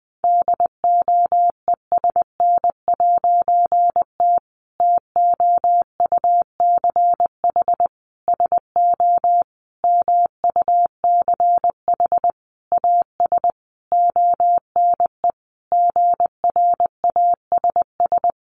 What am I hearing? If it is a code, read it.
Morse code, heard as a keyed tone, about 20 wpm, 700 Hz. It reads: DOESN'T TOUCH SO MUCH AS ONE GRASS
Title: A Sound Of Thunder 20wpm